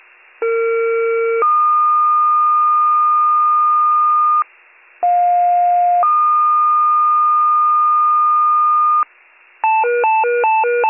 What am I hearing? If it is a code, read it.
Two-tone page, heard as a one-second tone, then a three-second tone, about 460 then 1220 Hz, about 710 then 1220 Hz. It is heard from 0.4 to 9.0 s.